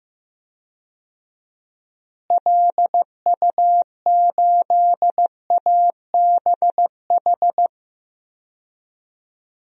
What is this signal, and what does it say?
Morse code, heard as a keyed tone, about 15 wpm, 700 Hz. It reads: LU8ABH
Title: DXCC-100 Call Signs - 6 Characters - Repeated 2x - 15wpm